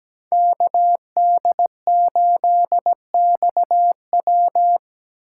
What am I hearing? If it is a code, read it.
Morse code, heard as a keyed tone, about 17 wpm, 700 Hz. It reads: KD8XW
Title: USA Call Signs - 5 Characters - 2x Character Spacing - 17wpm